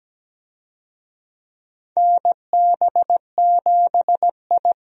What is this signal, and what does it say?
Morse code, heard as a keyed tone, about 17 wpm, 700 Hz. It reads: NB7I
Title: USA Call Signs - 4 Characters - Repeated 3x - 17wpm